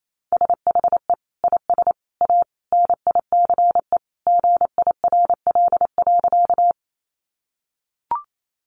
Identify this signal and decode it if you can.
Morse code, heard as a keyed tone, about 28 wpm, 700 Hz. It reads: SHE IS A NICE GIRL.
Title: Sentences from Top 1100 Words 28wpm